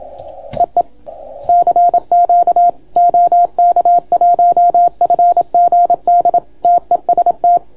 Morse code, heard as a keyed tone, about 27 wpm, 690 Hz. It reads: EE CQ OK1FGD TEST